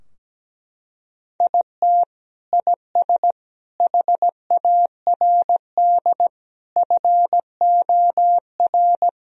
Morse code, heard as a keyed tone, about 17 wpm, 700 Hz. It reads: IT IS HARD FOR
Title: Sentences from Top 1000 Words - Review Part 3 - 17wpm